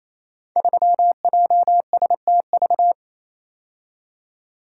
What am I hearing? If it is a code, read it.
Morse code, heard as a keyed tone, about 27 wpm, 700 Hz. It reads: 3JSTV
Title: Code Groups - 5 Letter:Numbers - 28wpm